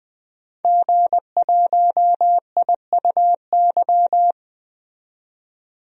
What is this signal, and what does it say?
Morse code, heard as a keyed tone, about 20 wpm, 700 Hz. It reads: G1IUY
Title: DXCC-100 Call Signs - 5 Characters - 20wpm